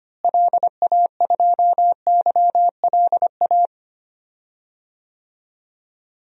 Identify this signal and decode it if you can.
Morse code, heard as a keyed tone, about 25 wpm, 700 Hz. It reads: LA2YLA
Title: DXCC-100 Call Signs - 6 Characters - Repeated 3x - 25wpm